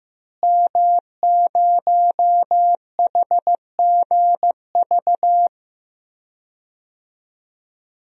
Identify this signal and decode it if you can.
Morse code, heard as a keyed tone, about 16 wpm, 700 Hz. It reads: M0HGV
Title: DXCC-100 Call Signs - 5 Characters - Repeated 3x - 15wpm